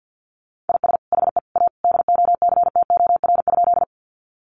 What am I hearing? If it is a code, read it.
Morse code, heard as a keyed tone, about 50 wpm, 700 Hz. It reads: IS HE A DOCTOR?
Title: Sentences from Top 1500 Words 50wpm